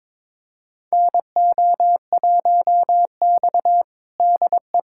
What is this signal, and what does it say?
Morse code, heard as a keyed tone, about 22 wpm, 700 Hz. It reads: NO1X DE
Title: USA Callsign DE Callsign - 4 Characters - 22wpm